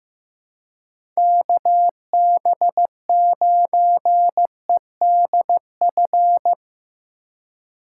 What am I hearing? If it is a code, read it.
Morse code, heard as a keyed tone, about 15 wpm, 700 Hz. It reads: KB9EDF
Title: USA Call Signs - 6 Characters - 3x Character Spacing - 15wpm